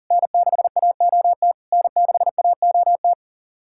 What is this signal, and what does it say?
Morse code, heard as a keyed tone, about 40 wpm, 700 Hz. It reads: N6AOT N6AOT
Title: SOTA QSO Practice 40wpm